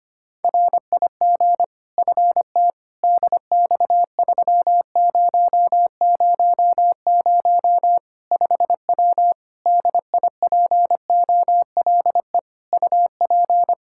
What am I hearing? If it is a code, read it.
Morse code, heard as a keyed tone, about 25 wpm, 700 Hz. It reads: RIG FT DX3000 5W DIPOLE UP
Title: Practice QSO - Exchange 2 - Info - 25wpm